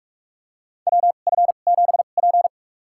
Morse code, heard as a keyed tone, about 46 wpm, spaced as for 32 wpm, 700 Hz. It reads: WF7P
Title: USA Call Signs - 2x Character Spacing - 45wpm